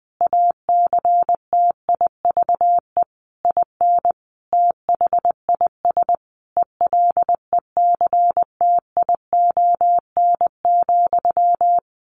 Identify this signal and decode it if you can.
Morse code, heard as a keyed tone, about 20 wpm, 700 Hz. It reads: ACTIVE IN THIS ELECTION,
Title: Sentences from Top 1500 Words 20wpm